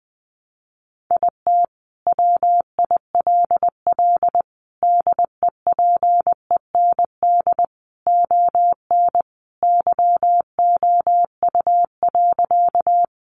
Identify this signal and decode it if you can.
Morse code, heard as a keyed tone, about 20 wpm, 700 Hz. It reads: IT WILL DEPEND ON YOU.
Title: Sentences from Top 1300 Words 20wpm